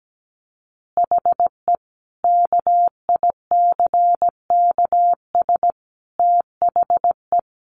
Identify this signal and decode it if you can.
Morse code, heard as a keyed tone, about 17 wpm, 700 Hz. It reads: HE KICKS THE
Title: Additional Easy Phrases 17wpm